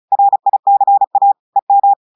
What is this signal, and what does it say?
Morse code, heard as a keyed tone, about 36 wpm, 800 Hz. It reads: RICA EM
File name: audios / portugais / portugais texte en clair 1 vitesse 35 mots minutes